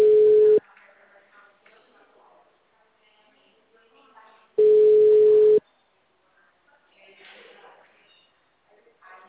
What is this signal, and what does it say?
Ringback tone (Europe, 425 Hz), with speech.